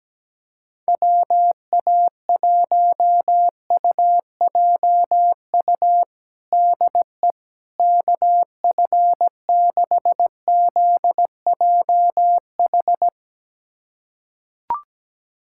Morse code, heard as a keyed tone, about 17 wpm, 700 Hz. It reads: WA1UJU DE KF6ZJH
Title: USA Callsign DE Callsign 17wpm